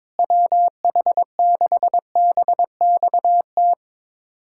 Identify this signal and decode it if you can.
Morse code, heard as a keyed tone, about 22 wpm, 700 Hz. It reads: WH6BXT
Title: USA Callsign DE Callsign 22wpm